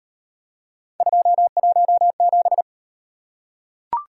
Morse code, heard as a keyed tone, about 34 wpm, 700 Hz. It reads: 217